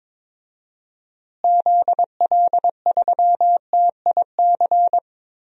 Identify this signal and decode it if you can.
Morse code, heard as a keyed tone, about 22 wpm, 700 Hz. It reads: ZL3TIC
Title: DXCC-100 Call Signs - 6 Characters - Repeated 2x - 22wpm